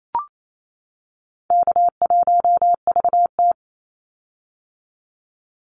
Morse code, heard as a keyed tone, about 28 wpm, 700 Hz. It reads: K1VT
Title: Call Signs - 28wpm